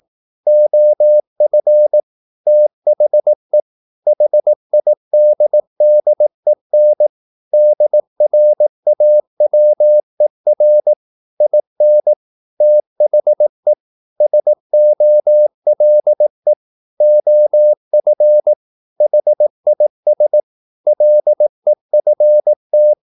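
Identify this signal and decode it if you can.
Morse code, heard as a keyed tone, about 18 wpm, 600 Hz. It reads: OF THE HIDDEN DRAWER IN THE SOLE OF HIS LEFT
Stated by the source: Morse practice at 18 wpm, English text, from CW med Gnister.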